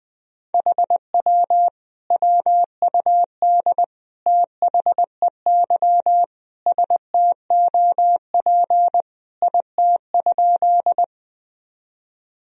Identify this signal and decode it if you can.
Morse code, heard as a keyed tone, about 20 wpm, 700 Hz. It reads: HW WUD THEY STOP IT?
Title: Sentences from Top 300 Words and Abbreviations 20wpm